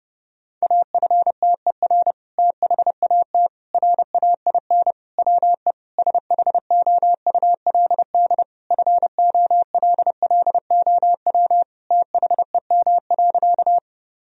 Morse code, heard as a keyed tone, about 30 wpm, 700 Hz. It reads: AFTER THAT RAIN WE SHOULD FOLLOW THEM.
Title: Sentences from Top 1000 Words - Review Part 1 - 30wpm